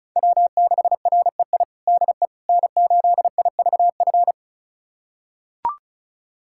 Morse code, heard as a keyed tone, about 34 wpm, 700 Hz. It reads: W6REI DE N8IVF
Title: USA Callsign DE Callsign 35wpm